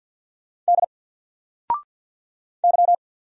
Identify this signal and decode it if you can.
Morse code, heard as a keyed tone, about 48 wpm, 700 Hz. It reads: D Y
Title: Single Letter-Number 50wpm